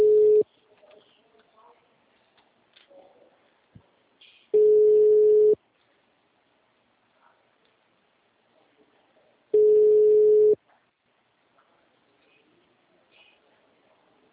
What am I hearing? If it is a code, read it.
Ringback tone (Europe, 425 Hz).